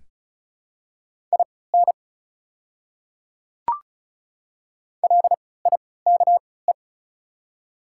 Morse code, heard as a keyed tone, about 36 wpm, spaced as for 20 wpm, 700 Hz. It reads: IN LIKE
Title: 3x Character Spacing - Top 100 Words - 35wpm